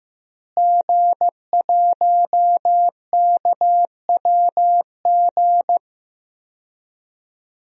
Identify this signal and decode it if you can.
Morse code, heard as a keyed tone, about 16 wpm, 700 Hz. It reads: G1KWG
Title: DXCC-100 Call Signs - 5 Characters - Repeated 2x - 15wpm